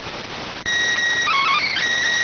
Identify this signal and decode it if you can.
SSTV (slow-scan television), PD 290.